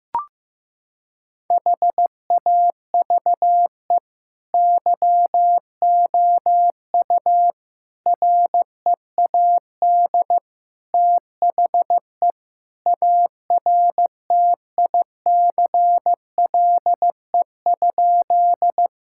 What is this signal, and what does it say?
Morse code, heard as a keyed tone, about 15 wpm, 700 Hz. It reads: HAVE YOU READ THE ARTICLE?
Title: Sentences from Top 1000 Words - Review Part 5 - 15wpm